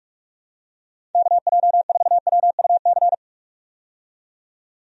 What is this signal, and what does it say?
Morse code, heard as a keyed tone, about 46 wpm, 700 Hz. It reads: KJ4WUC
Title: USA Call Signs - 6 Characters - 45wpm